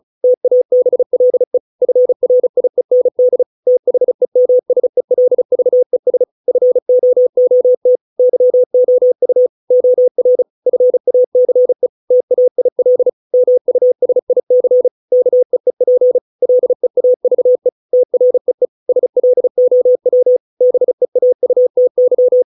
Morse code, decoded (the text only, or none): TABLE FRIEND THEMSELVES FOOT YOU OR FACE TAIL MUSIC KEEP LEAVE TREE SLOW BEAUTY